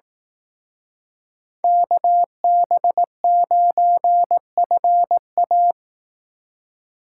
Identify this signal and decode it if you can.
Morse code, heard as a keyed tone, about 18 wpm, 700 Hz. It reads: KB9FA
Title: Speed Racing - USA 2x2 Call Signs 15wpm